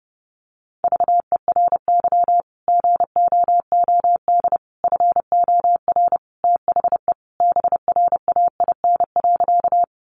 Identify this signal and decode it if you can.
Morse code, heard as a keyed tone, about 30 wpm, 700 Hz. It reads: VERY GOOD FOR THE BRAIN.
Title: Sentences from Top 1100 Words 30wpm